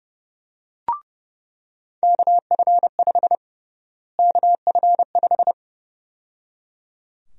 Morse code, heard as a keyed tone, about 32 wpm, 700 Hz. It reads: KF5 KF5
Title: USA Call Sign Prefixes - Repeated 2x - 30wpm